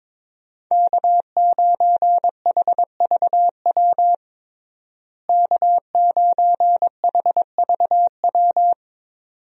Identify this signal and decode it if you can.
Morse code, heard as a keyed tone, about 22 wpm, 700 Hz. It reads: K9HVW K9HVW
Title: USA Call Signs - 5 Characters - Repeated 2x - 22wpm